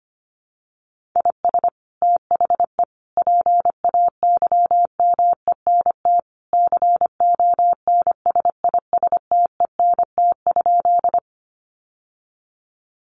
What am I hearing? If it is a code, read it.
Morse code, heard as a keyed tone, about 25 wpm, 700 Hz. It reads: IS THE PAYMENT CONSISTENT?
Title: Sentences from Top 1500 Words 25wpm